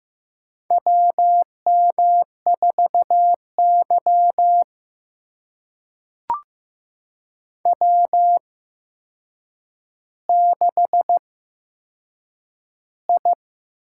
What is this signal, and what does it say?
Morse code, heard as a keyed tone, about 15 wpm, spaced as for 7 wpm, 700 Hz. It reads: WM4Y W6I
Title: USA Call Signs - 8x Character Spacing - 15wpm